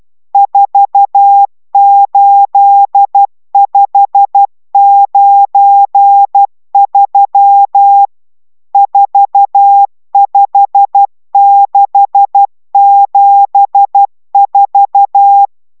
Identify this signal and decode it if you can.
Morse code, heard as a keyed tone, about 12 wpm, 800 Hz. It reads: 48593 45674